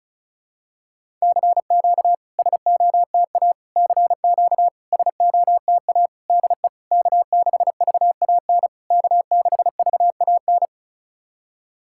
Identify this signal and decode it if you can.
Morse code, heard as a keyed tone, about 34 wpm, 660 Hz. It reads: CQ SOTA CQ SOTA DE K6VAN K6VAN
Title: SOTA QSO Practice 35wpm